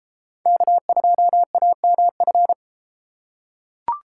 Morse code, heard as a keyed tone, about 32 wpm, 700 Hz. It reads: K2AMF